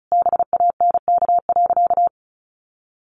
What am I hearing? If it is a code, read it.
Morse code, heard as a keyed tone, about 34 wpm, 700 Hz. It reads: BANK.
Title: Sentences from Top 1500 Words 35wpm